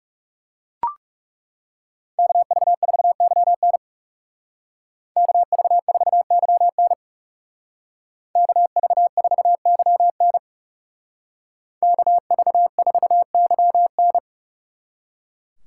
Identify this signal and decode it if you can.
Morse code, heard as a keyed tone, about 39 wpm, 700 Hz. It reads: KV4YN KV4YN KV4YN KV4YN